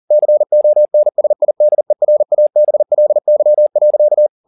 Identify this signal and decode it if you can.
Morse code, heard as a keyed tone, about 40 wpm, 600 Hz. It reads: CONSIDERABLY.
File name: the old man and the sea 2 40wpm